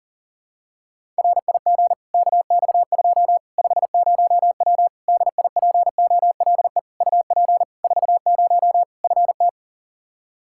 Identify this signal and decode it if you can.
Morse code, heard as a keyed tone, about 40 wpm, 700 Hz. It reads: RIG KX2 50W DIPOLE UP 40 FT
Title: Practice QSO - Exchange 2 - Info - 40wpm